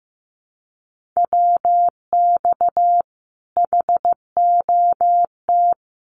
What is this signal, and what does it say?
Morse code, heard as a keyed tone, about 15 wpm, 700 Hz. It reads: WX HOT
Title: Common QSO Phrases Encore 15wpm